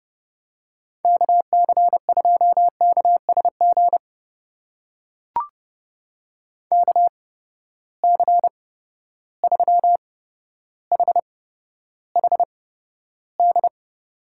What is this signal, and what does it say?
Morse code, heard as a keyed tone, about 30 wpm, spaced as for 13 wpm, 700 Hz. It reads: KC2KSG KC3HHD
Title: USA Call Signs - 6 Characters - 8x Character Spacing - 30wpm